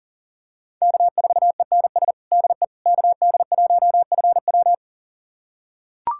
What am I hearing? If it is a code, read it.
Morse code, heard as a keyed tone, about 37 wpm, 700 Hz. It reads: K4ENS DE KD1FW
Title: USA Callsign DE Callsign 40wpm